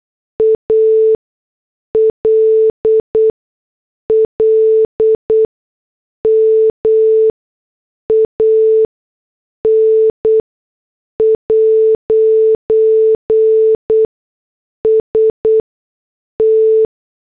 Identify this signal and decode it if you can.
Morse code, heard as a keyed tone, about 8 wpm, spaced as for 6 wpm, 440 Hz. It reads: ALLMAN'S T
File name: morse / headlines